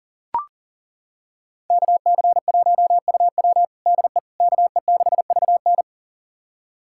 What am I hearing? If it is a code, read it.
Morse code, heard as a keyed tone, about 40 wpm, 700 Hz. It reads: KC1UW DE KE6VN
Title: USA Callsign DE Callsign - 5 Characters - 40wpm